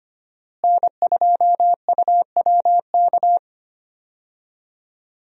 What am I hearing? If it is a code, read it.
Morse code, heard as a keyed tone, about 25 wpm, 700 Hz. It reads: N2UWK
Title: USA Call Signs - 5 Characters - Repeated 3x - 25wpm